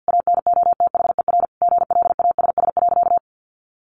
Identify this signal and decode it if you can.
Morse code, heard as a keyed tone, about 50 wpm, 700 Hz. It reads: ANOTHER GLASS.
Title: Sentences from Top 1500 Words 50wpm